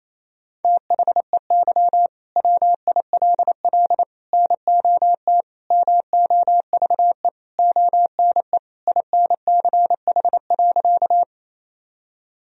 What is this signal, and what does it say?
Morse code, heard as a keyed tone, about 28 wpm, 700 Hz. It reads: THEY WILL NOT MOVE ONE INCH.
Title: Sentences from Top 1000 Words - Review Part 3 - 28wpm